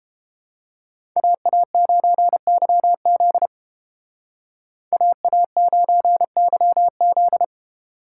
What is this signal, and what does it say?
Morse code, heard as a keyed tone, about 33 wpm, 700 Hz. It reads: AA9YZ AA9YZ